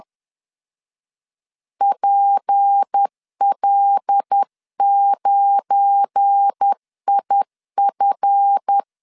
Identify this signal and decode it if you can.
Morse code, heard as a keyed tone, about 10 wpm, 800 Hz. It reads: PL9IF
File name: CW / DET / 1 9 1500C S A 10wpm 4511